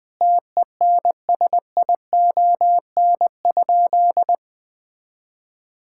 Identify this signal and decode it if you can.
Morse code, heard as a keyed tone, about 20 wpm, 700 Hz. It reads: TENSION?
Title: Sentences from Top 1600 Words 20wpm